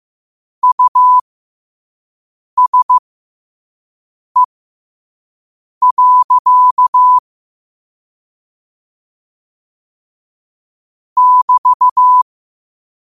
Morse code, heard as a keyed tone, about 15 wpm, spaced as for 5 wpm, 1000 Hz. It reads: USE. <BT>
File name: qotdfiles / 5wpm